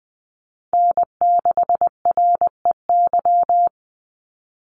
Morse code, heard as a keyed tone, about 19 wpm, 700 Hz. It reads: N6REY
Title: USA Call Signs 20wpm